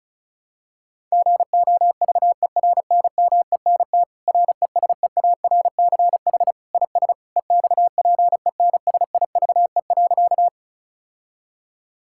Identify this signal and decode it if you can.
Morse code, heard as a keyed tone, about 36 wpm, 700 Hz. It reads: GOVERNMENT RESEARCH IS EXPENSIVE.
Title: Sentences from Top 1000 Words - Review Part 6 - 35wpm